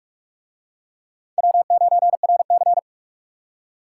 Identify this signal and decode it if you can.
Morse code, heard as a keyed tone, about 44 wpm, 700 Hz. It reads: W9RC